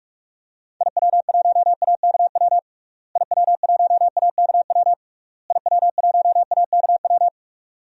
Morse code, heard as a keyed tone, about 41 wpm, 700 Hz. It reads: IW1AKW IW1AKW IW1AKW
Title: DXCC-100 Call Signs - 6 Characters - Repeated 3x - 45wpm